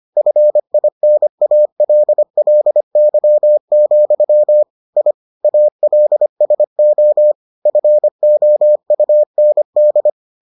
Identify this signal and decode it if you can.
Morse code, heard as a keyed tone, about 25 wpm, 600 Hz. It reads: FINALLY, I ALSO FOUND